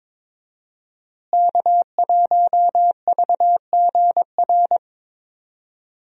Morse code, heard as a keyed tone, about 22 wpm, 700 Hz. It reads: K1VGR
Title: USA Call Signs - 5 Characters - Repeated 2x - 22wpm